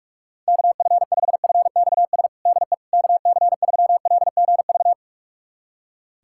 Morse code, heard as a keyed tone, about 46 wpm, 700 Hz. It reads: KF5FXS DE KC3LGV
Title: USA Callsign DE Callsign - 6 Characters - 45wpm